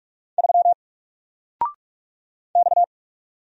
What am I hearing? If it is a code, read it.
Morse code, heard as a keyed tone, about 44 wpm, 700 Hz. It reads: 3 X
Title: Single Letter-Number Alt 45wpm